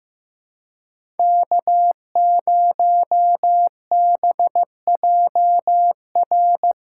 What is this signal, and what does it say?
Morse code, heard as a keyed tone, about 15 wpm, 700 Hz. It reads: K0BJR
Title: USA Call Signs - 5 Characters - 4x Character Spacing - 15wpm